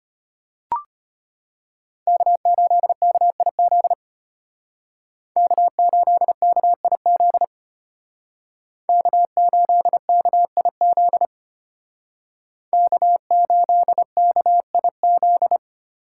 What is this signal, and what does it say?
Morse code, heard as a keyed tone, about 33 wpm, 700 Hz. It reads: K8KIZ K8KIZ K8KIZ K8KIZ